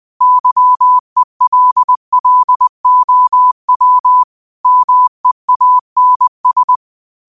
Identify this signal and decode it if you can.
Morse code, heard as a keyed tone, about 20 wpm, 1000 Hz. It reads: YELLOW MEANS
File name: qotdfiles / 20wpm